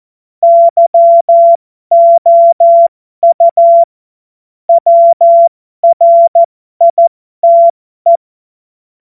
Morse code, heard as a keyed tone, about 14 wpm, spaced as for 12 wpm, 680 Hz. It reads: YOU WRITE
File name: Morsekurs / Witze